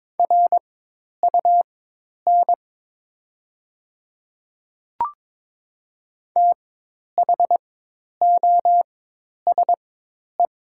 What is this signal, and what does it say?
Morse code, heard as a keyed tone, about 22 wpm, spaced as for 10 wpm, 700 Hz. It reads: RUN THOSE
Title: 4x Character Spacing - Top 500 Words - 22wpm